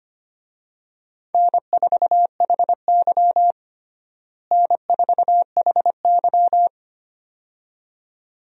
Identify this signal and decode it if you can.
Morse code, heard as a keyed tone, about 25 wpm, 700 Hz. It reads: N4HY N4HY
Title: USA Call Signs - 4 Characters - Repeated 2x - 25wpm